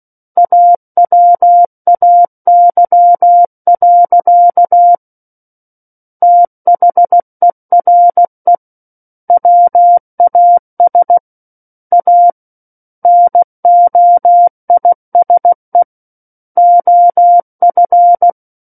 Morse code, heard as a keyed tone, about 16 wpm, 700 Hz. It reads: AWAY. THERE WAS A NOISE OF
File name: Book One / 07-Chapter 7 - 16 WPM